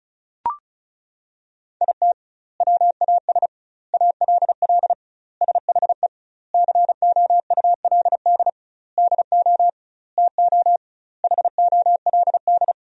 Morse code, heard as a keyed tone, about 36 wpm, 700 Hz. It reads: IT WAS ALL SHE COULD DO TO HOLD
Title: 2x Word-Spacing - Sentences from Top 400 Words - 35wpm